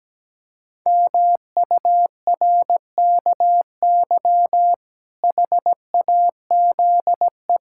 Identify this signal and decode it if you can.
Morse code, heard as a keyed tone, about 17 wpm, 700 Hz. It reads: MURKY HAZE
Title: Pangram Challenge v2 - 17wpm